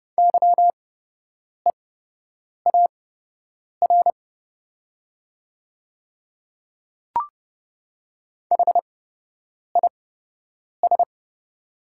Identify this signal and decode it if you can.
Morse code, heard as a keyed tone, about 30 wpm, spaced as for 8 wpm, 700 Hz. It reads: YEAR HIS
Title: 8x Character Spacing - Top 300 Words - 30wpm